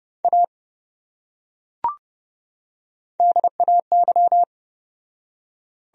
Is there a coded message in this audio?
Yes: Morse code from 0.2 to 4.4 s, heard as a keyed tone, about 28 wpm, 700 Hz. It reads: A DAY